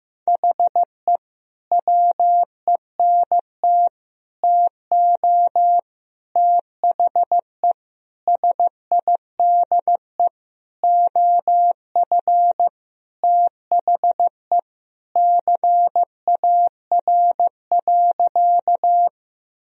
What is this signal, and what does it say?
Morse code, heard as a keyed tone, about 15 wpm, 700 Hz. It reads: HE WENT TO THE SIDE OF THE CAR.
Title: Sentences from Top 1000 Words - Review Part 1 - 15wpm